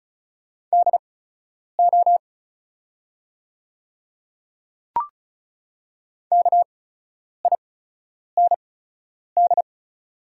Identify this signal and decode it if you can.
Morse code, heard as a keyed tone, about 34 wpm, spaced as for 10 wpm, 700 Hz. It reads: DO KIND
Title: 8x Character Spacing - Top 300 Words - 35wpm